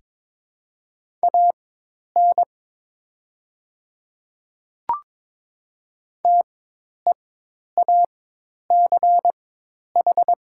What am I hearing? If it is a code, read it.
Morse code, heard as a keyed tone, about 22 wpm, spaced as for 10 wpm, 700 Hz. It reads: AN TEACH
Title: 4x Character Spacing - Top 500 Words - 22wpm